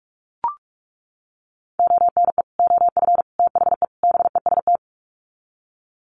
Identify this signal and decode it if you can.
Morse code, heard as a keyed tone, about 44 wpm, 700 Hz. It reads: ONE OF THE BEST
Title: Sets of 4 Words 45wpm